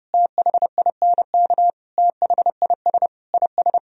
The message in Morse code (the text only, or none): THINK THIS IS